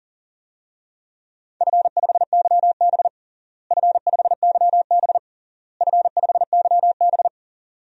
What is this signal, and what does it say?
Morse code, heard as a keyed tone, about 40 wpm, 700 Hz. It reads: F5YB F5YB F5YB
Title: DXCC-100 Call Signs - 4 Characters - Repeated 3x - 40wpm